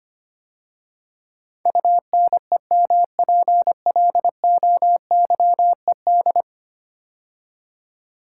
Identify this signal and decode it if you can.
Morse code, heard as a keyed tone, about 25 wpm, 700 Hz. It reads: UNEMPLOYED
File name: 10 Letter Words - First 3 Letters Given - 25wpm